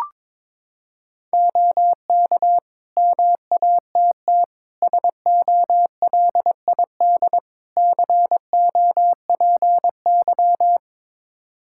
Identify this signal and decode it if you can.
Morse code, heard as a keyed tone, about 22 wpm, 700 Hz. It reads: OK MATT SOLID COPY
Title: Practice QSO - Exchange 3 - More Info - 22wpm